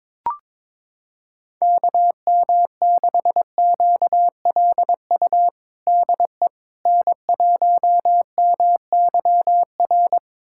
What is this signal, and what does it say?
Morse code, heard as a keyed tone, about 21 wpm, 700 Hz. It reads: KM6QLU DE N1MYR
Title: USA Callsign DE Callsign 22wpm